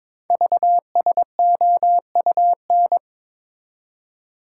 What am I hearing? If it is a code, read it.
Morse code, heard as a keyed tone, about 23 wpm, 700 Hz. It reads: VSOUN
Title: Code Groups - 5 Letters - 22wpm